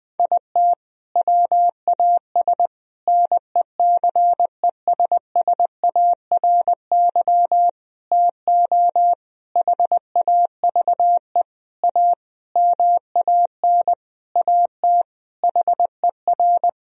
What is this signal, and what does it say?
Morse code, heard as a keyed tone, about 20 wpm, 700 Hz. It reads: IT WAS NECESSARY TO HAVE A MAN AT HER
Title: Sentences from Top 1000 Words - Review Part 1 - 20wpm